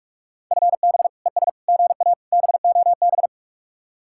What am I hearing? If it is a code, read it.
Morse code, heard as a keyed tone, about 45 wpm, 700 Hz. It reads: FB ES GA BOB